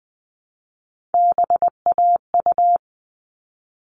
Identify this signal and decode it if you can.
Morse code, heard as a keyed tone, about 20 wpm, 700 Hz. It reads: BAU
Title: All Cut Numbers - 3-Digit 20wpm